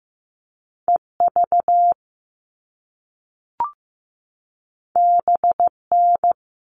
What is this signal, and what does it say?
Morse code, heard as a keyed tone, about 15 wpm, 700 Hz. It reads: EV BN
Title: All Cut Numbers - 2-Digit 15wpm